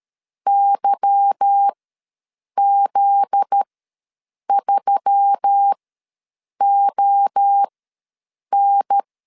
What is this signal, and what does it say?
Morse code, heard as a keyed tone, about 13 wpm, spaced as for 7 wpm, 800 Hz. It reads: YZ3ON
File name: LEZIONI / 8 YJ45 / 8 YJ4 ws 12wpm 3249